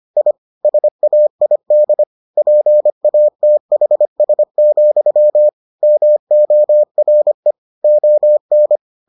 Morse code, heard as a keyed tone, about 25 wpm, 600 Hz. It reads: I SAID PATHS, MORE ON